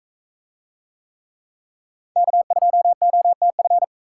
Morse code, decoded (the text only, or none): K2OTF